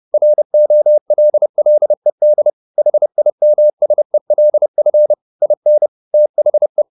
Morse code, heard as a keyed tone, about 30 wpm, 600 Hz. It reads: ROLLED HIMSELF IN THE
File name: the old man and the sea / the old man and the sea 1 30wpm